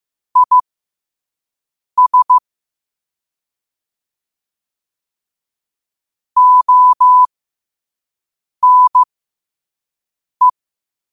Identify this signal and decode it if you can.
Morse code, heard as a keyed tone, about 15 wpm, spaced as for 5 wpm, 1000 Hz. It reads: IS ONE
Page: New quotes every day in morse code at 5 Words per minute.